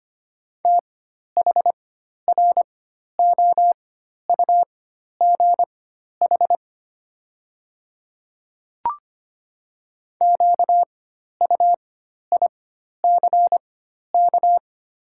Morse code, heard as a keyed tone, about 25 wpm, spaced as for 12 wpm, 700 Hz. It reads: THROUGH QUICK
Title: 4x Character Spacing - Top 500 Words - 25wpm